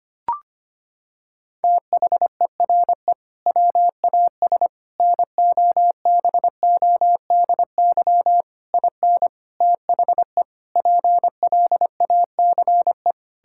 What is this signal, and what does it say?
Morse code, heard as a keyed tone, about 25 wpm, 700 Hz. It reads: THERE WAS NOBODY IN THE PLACE
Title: Sentences from Top 1600 Words 25wpm